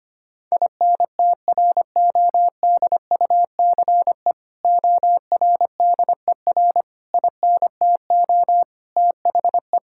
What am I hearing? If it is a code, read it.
Morse code, heard as a keyed tone, about 25 wpm, 700 Hz. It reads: INTRODUCE ORDER INTO THE
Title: Sentences from Top 1600 Words 25wpm